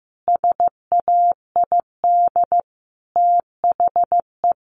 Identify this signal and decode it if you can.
Morse code, heard as a keyed tone, about 15 wpm, 700 Hz. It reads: SAID THE
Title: Sentences from Top 1500 Words 15wpm